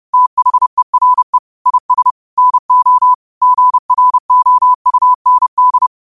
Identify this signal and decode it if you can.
Morse code, heard as a keyed tone, about 30 wpm, 1000 Hz. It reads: THERE IS NO GROUND
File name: qotdfiles / 30wpm